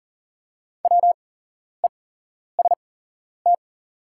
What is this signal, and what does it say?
Morse code, heard as a keyed tone, about 40 wpm, spaced as for 11 wpm, 690 Hz. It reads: WEST
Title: 8x Character Spacing - Top 500 Words - 40wpm